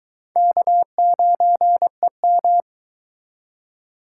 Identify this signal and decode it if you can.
Morse code, heard as a keyed tone, about 24 wpm, 700 Hz. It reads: K9EM